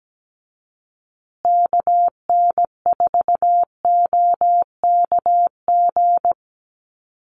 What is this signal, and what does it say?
Morse code, heard as a keyed tone, about 17 wpm, 700 Hz. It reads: KN4OKG
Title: USA Call Signs Repeated 3x 17wpm